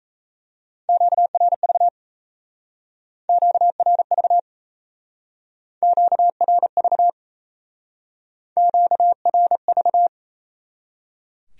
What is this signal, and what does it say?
Morse code, heard as a keyed tone, about 34 wpm, 700 Hz. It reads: QRV QRV QRV QRV
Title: Speed-Racing Q-Codes - 28wpm